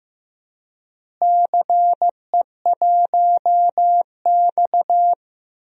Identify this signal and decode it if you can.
Morse code, heard as a keyed tone, about 15 wpm, 700 Hz. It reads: CE1X
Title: DXCC-100 Call Signs - 4 Characters - Repeated 2x - 15wpm